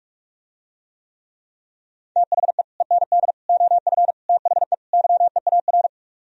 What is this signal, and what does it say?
Morse code, heard as a keyed tone, about 48 wpm, 700 Hz. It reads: THE END OF THE YEAR